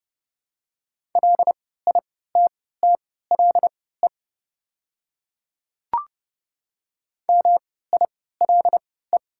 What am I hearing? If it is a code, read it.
Morse code, heard as a keyed tone, about 30 wpm, spaced as for 17 wpm, 700 Hz. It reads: LITTLE MILE
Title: 3x Character Spacing - Top 300 Words - 30wpm